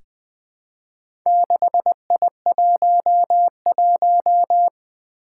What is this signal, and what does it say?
Morse code, heard as a keyed tone, about 20 wpm, 700 Hz. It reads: 6I11
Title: Four Challenging Letter-Number Combinations 20wpm